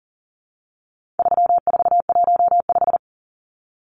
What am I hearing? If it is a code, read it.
Morse code, heard as a keyed tone, about 40 wpm, 700 Hz. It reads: 3415
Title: Numbers - 4-Digit 40wpm